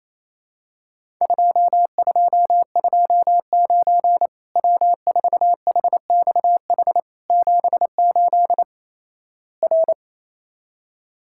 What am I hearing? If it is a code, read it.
Morse code, heard as a keyed tone, about 28 wpm, 700 Hz. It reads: 2229 W4HXH 78 R
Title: Worked All Europe DX Contest - QTC Exchange - 28wpm